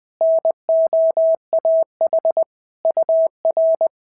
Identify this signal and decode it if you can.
Morse code, heard as a keyed tone, about 20 wpm, 650 Hz. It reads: NOAH UR
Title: SOTA QSO Practice 20wpm